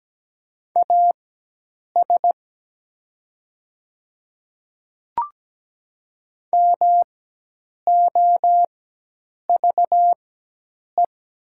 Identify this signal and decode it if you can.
Morse code, heard as a keyed tone, about 17 wpm, spaced as for 8 wpm, 700 Hz. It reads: AS MOVE
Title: 4x Character Spacing - Top 200 Words - 17wpm